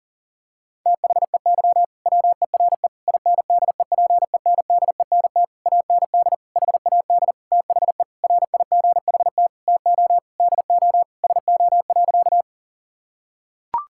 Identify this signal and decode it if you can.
Morse code, heard as a keyed tone, about 37 wpm, 690 Hz. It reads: THEY WERE INDEPENDENT AND HAD THE RIGHT TO DO SO.
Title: Sentences from Top 1000 Words - Review Part 6 - 40wpm